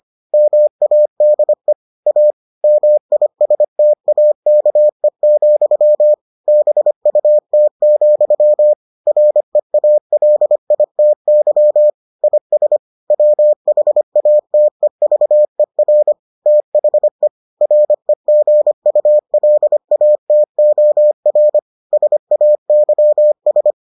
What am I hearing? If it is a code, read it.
Morse code, heard as a keyed tone, about 25 wpm, 600 Hz. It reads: MADE A MISTAKE, BUT, REALITY IS WHATEVER THE REGULATOR SAYS